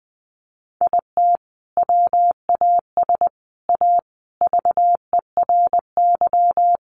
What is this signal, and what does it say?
Morse code, heard as a keyed tone, about 20 wpm, 700 Hz. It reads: IT WAS A VERY
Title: Sentences from Top 1400 Words 20wpm